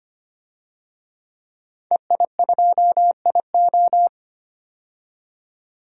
Morse code, heard as a keyed tone, about 25 wpm, 700 Hz. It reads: EI2IO
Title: DXCC-100 Call Signs - 5 Characters - 25wpm